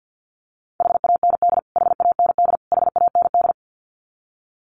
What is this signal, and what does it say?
Morse code, heard as a keyed tone, about 50 wpm, 700 Hz. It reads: HAND HAND HAND
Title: Top 500 Words Repeated 50wpm